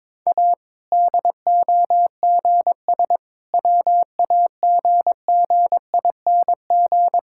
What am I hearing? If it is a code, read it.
Morse code, heard as a keyed tone, about 22 wpm, 700 Hz. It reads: A DOGS WAGGING
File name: Difficult Sentences with Hints - 22wpm